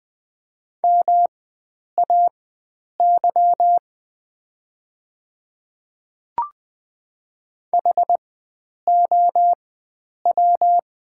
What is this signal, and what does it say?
Morse code, heard as a keyed tone, about 20 wpm, spaced as for 9 wpm, 690 Hz. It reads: MAY HOW
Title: 4x Character Spacing - Top 100 Words - 20wpm